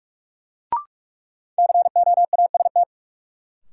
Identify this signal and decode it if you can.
Morse code, heard as a keyed tone, about 44 wpm, 700 Hz. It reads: COAST
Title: Sentences from Top 1300 Words - New Word Drill Alt - 45wpm